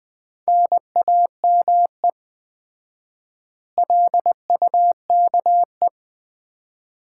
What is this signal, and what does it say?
Morse code, heard as a keyed tone, about 20 wpm, 700 Hz. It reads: NAME LUKE
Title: Common QSO Phrases - 4x Word-Spacing - 20wpm